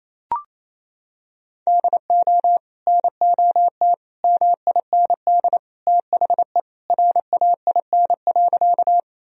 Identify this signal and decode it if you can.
Morse code, heard as a keyed tone, about 28 wpm, 700 Hz. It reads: DO NOT MIND THE RAIN.
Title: Sentences from Top 1000 Words - Review Part 1 - 28wpm